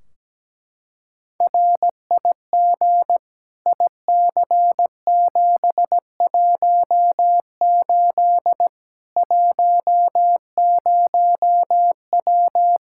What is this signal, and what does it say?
Morse code, heard as a keyed tone, about 17 wpm, 700 Hz. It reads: RIG IC718 10W
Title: Practice QSO - Exchange 2 - Info - 17wpm